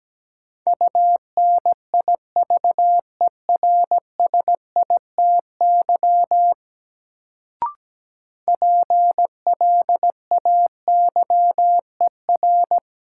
Morse code, heard as a keyed tone, about 17 wpm, 700 Hz. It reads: UNIVERSITY PLAYER
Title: Sentences from Top 1000 Words - Review Part 4 - New Word Drill - 17wpm